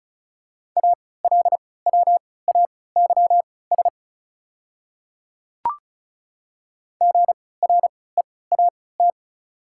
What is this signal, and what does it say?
Morse code, heard as a keyed tone, about 34 wpm, spaced as for 20 wpm, 690 Hz. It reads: ALWAYS GREAT
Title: 3x Character Spacing - Top 500 Words - 35wpm